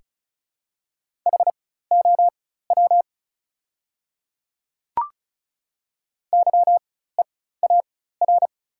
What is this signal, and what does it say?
Morse code, heard as a keyed tone, about 36 wpm, spaced as for 16 wpm, 700 Hz. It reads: HOW YEAR
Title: 4x Character Spacing - Top 300 Words - 35wpm